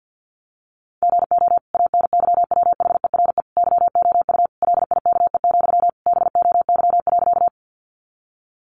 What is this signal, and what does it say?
Morse code, heard as a keyed tone, about 50 wpm, 700 Hz. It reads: GO ANYWHERE YOU LIKE, BOY.
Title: Sentences from Top 1400 Words 50wpm